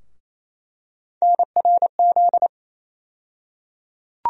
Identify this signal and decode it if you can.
Morse code, heard as a keyed tone, about 28 wpm, 700 Hz. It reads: NRZ
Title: USA Call Sign Postfixes - 3 Letters 28wpm